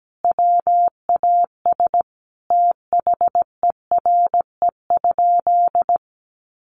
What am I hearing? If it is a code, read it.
Morse code, heard as a keyed tone, about 17 wpm, 700 Hz. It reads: WAS THERE?
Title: Sentences from Top 1200 Words 17wpm